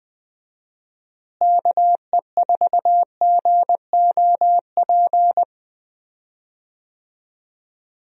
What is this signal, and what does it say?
Morse code, heard as a keyed tone, about 20 wpm, 700 Hz. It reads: KE4GOP
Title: USA Call Signs - 6 Characters - Repeated 3x - 20wpm